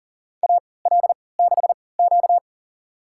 Morse code, heard as a keyed tone, about 40 wpm, spaced as for 23 wpm, 700 Hz. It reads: AL6Q
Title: USA Call Signs - 4 Characters - 3x Character Spacing - 40wpm